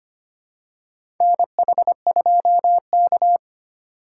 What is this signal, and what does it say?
Morse code, heard as a keyed tone, about 25 wpm, 700 Hz. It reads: NH2K
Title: USA Call Signs - 4 Characters - 4x Character Spacing - 25wpm